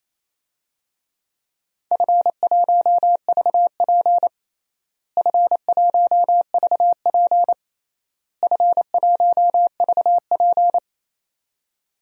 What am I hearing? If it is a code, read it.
Morse code, heard as a keyed tone, about 28 wpm, 700 Hz. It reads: F1VP F1VP F1VP
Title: DXCC-100 Call Signs - 4 Characters - Repeated 3x - 28wpm